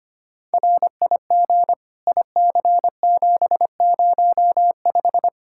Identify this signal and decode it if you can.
Morse code, heard as a keyed tone, about 25 wpm, 700 Hz. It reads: RIG IC705
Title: Practice QSO - Exchange 2 - Info - 25wpm